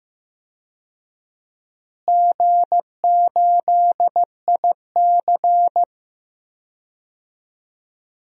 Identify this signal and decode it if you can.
Morse code, heard as a keyed tone, about 15 wpm, 700 Hz. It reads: G8IC
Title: DXCC-100 Call Signs - 4 Characters - Repeated 2x - 15wpm